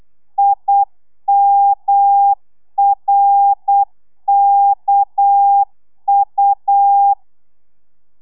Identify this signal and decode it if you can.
Morse code, heard as a keyed tone, about 8 wpm, 800 Hz. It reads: IMRKU